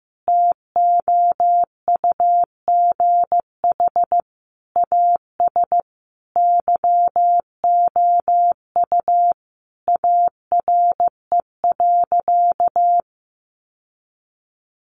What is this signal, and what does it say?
Morse code, heard as a keyed tone, about 15 wpm, 700 Hz. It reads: TOUGH AS YOU ARE.
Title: Sentences from Top 1200 Words 15wpm